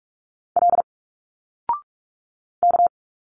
Morse code, heard as a keyed tone, about 46 wpm, 700 Hz. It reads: L K
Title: Single Letters 45wpm